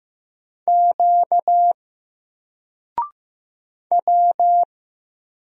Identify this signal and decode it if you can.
Morse code, heard as a keyed tone, about 15 wpm, 700 Hz. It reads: Q W
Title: Single Character (Letter:Number:Punctuation:Prosign) Alt 15wpm